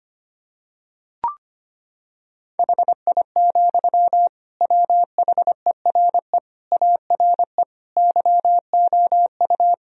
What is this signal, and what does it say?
Morse code, heard as a keyed tone, about 25 wpm, 700 Hz. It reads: HI, WHERE ARE YOU
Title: Sentences from Top 1000 Words - Review Part 3 - 25wpm